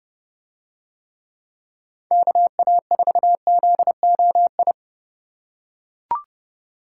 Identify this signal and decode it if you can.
Morse code, heard as a keyed tone, about 28 wpm, 700 Hz. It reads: KA4ZOI